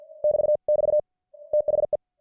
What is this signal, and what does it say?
Morse code, heard as a keyed tone, about 50 wpm, 600 Hz. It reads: <BT><BT> THE